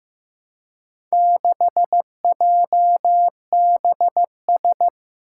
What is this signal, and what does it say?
Morse code, heard as a keyed tone, about 15 wpm, 700 Hz. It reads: 6JBS
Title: Four Challenging Letter-Number Combinations 15wpm